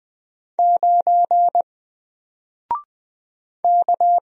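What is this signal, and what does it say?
Morse code, heard as a keyed tone, about 20 wpm, 700 Hz. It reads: 9 K
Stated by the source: Single letter and number Morse drill at 20 wpm, from N4LCD.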